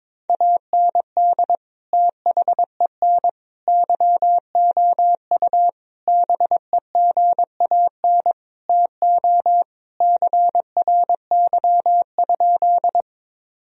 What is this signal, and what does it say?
Morse code, heard as a keyed tone, about 22 wpm, 700 Hz. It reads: AND THEN YOU BEGAN TO CRY?
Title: Sentences from Top 1000 Words - Review Part 2 - 22wpm